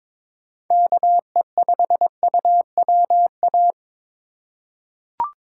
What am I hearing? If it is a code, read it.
Morse code, heard as a keyed tone, about 22 wpm, 700 Hz. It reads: KE5UWA
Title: USA Call Signs - 6 Characters - 4x Character Spacing - 22wpm